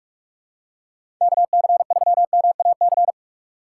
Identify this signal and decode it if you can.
Morse code, heard as a keyed tone, about 46 wpm, 700 Hz. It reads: KC3MAC
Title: USA Call Signs - 6 Characters - 2x Character Spacing - 45wpm